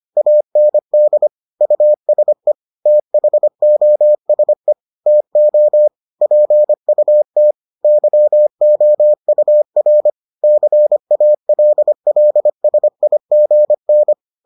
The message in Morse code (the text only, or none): AND USE THOSE TO PUT YOUR CALLSIGN